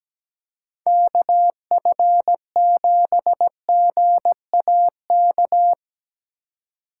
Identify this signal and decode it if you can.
Morse code, heard as a keyed tone, about 19 wpm, 700 Hz. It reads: KF7GAK
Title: Speed Racing - USA 2x3 Call Signs 17wpm